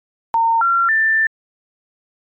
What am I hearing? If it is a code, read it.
Special information tone.
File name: Clean Phone Tones 24371